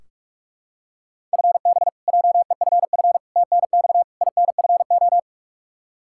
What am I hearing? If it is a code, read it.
Morse code, heard as a keyed tone, about 46 wpm, 700 Hz. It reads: FB JEFF TNX INFO
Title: Practice QSO - Exchange 3 - More Info - 45wpm